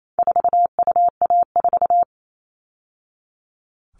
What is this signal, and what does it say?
Morse code, heard as a keyed tone, about 28 wpm, 700 Hz. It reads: 4UA4
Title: All Cut Numbers - 4-Digit 28wpm